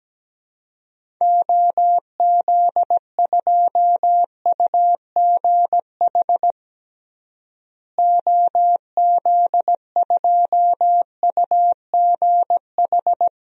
Morse code, heard as a keyed tone, about 17 wpm, 700 Hz. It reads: OZ2UGH OZ2UGH
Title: DXCC-100 Call Signs - 6 Characters - Repeated 3x - 17wpm